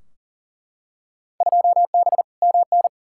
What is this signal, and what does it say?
Morse code, heard as a keyed tone, about 40 wpm, 700 Hz. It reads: 2B MN
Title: Field Day Exchanges - 40wpm